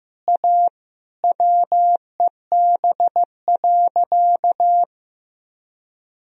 Morse code, heard as a keyed tone, about 15 wpm, 700 Hz. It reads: A WEB.
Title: Sentences from Top 1000 Words - Review Part 4 - 15wpm